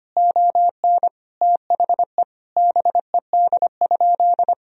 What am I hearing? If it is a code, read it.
Morse code, heard as a keyed tone, about 25 wpm, 700 Hz. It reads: ON THE BED?
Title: Sentences from Top 1000 Words - Review Part 4 - 25wpm